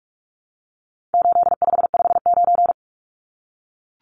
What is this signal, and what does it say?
Morse code, heard as a keyed tone, about 44 wpm, 700 Hz. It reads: 8559
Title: Numbers - 4-Digit 45wpm